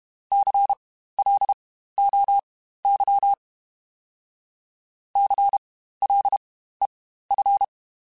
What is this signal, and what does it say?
Morse code, heard as a keyed tone, about 32 wpm, spaced as for 15 wpm, 800 Hz. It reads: CLOY CLEF
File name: Spaced Alphas / Spaced Alphas mp3s / Cs sp @30